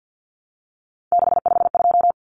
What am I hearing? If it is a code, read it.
Morse code, heard as a keyed tone, about 48 wpm, 700 Hz. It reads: 652
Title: Common Cut Numbers - 3-Digit 50wpm